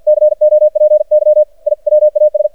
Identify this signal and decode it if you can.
Morse code, heard as a keyed tone, about 52 wpm, 610 Hz. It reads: COPY I WAS
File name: archived / CRSnet